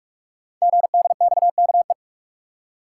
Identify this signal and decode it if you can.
Morse code, heard as a keyed tone, about 44 wpm, 700 Hz. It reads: GDXKE
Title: Code Groups - 5 Letters - 45wpm